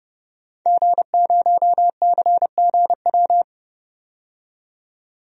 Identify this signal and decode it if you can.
Morse code, heard as a keyed tone, about 30 wpm, 700 Hz. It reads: G0CGW
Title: DXCC-100 Call Signs - 5 Characters - 30wpm